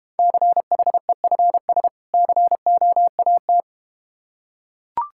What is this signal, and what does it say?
Morse code, heard as a keyed tone, about 32 wpm, 700 Hz. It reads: CHEFS COAT